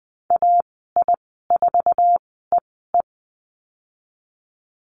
Morse code, heard as a keyed tone, about 20 wpm, spaced as for 14 wpm, 700 Hz. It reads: AI4EE
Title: 122 Lesson 20g - Number 3 - Call Signs 20wpm